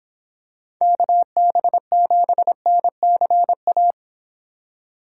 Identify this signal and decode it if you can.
Morse code, heard as a keyed tone, about 26 wpm, 700 Hz. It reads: KB7NCA